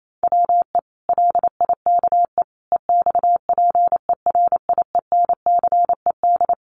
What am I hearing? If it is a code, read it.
Morse code, heard as a keyed tone, about 28 wpm, 700 Hz. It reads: WE LIKE EXPERIENCED
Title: Sentences from Top 1100 Words 28wpm